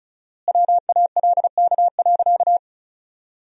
Morse code, heard as a keyed tone, about 34 wpm, 700 Hz. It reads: WALK.
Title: 4x Word-Spacing - Sentences from Top 300 Words - 35wpm